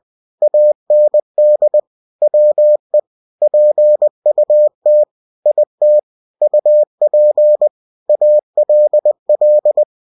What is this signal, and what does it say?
Morse code, heard as a keyed tone, about 20 wpm, 600 Hz. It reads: AND WE PUT IT UP ALL
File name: winnie-20wpm